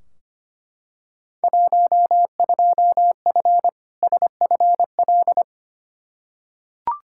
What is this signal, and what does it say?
Morse code, heard as a keyed tone, about 25 wpm, 700 Hz. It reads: 12F SFL
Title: Field Day Exchanges Encore - 25wpm